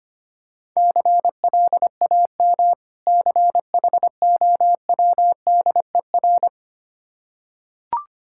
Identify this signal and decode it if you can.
Morse code, heard as a keyed tone, about 25 wpm, 690 Hz. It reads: CLAM CHOWDER
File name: practice / 25wpm / Difficult Words with Hints - 25wpm